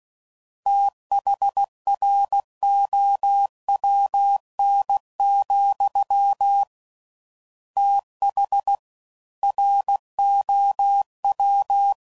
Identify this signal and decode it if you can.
Morse code, heard as a keyed tone, about 16 wpm, 800 Hz. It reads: THROWN, TH ROW
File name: Prefixes / FLs & THs / Fast TH